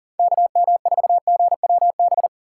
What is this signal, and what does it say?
Morse code, heard as a keyed tone, about 40 wpm, 700 Hz. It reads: KM4GWB
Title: USA Call Signs - 6 Characters - 4x Character Spacing - 40wpm